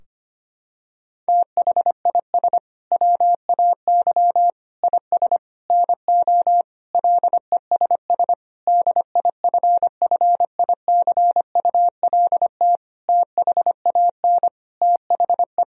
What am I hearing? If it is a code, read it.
Morse code, heard as a keyed tone, about 25 wpm, 700 Hz. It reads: THIS WAY IS NO LESS DIFFICULT THAN THE
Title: Sentences from Top 1000 Words - Review Part 2 - 25wpm